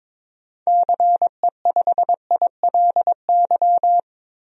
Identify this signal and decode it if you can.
Morse code, heard as a keyed tone, about 21 wpm, 700 Hz. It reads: CE5ILY
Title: DXCC-100 Call Signs - 6 Characters - 22wpm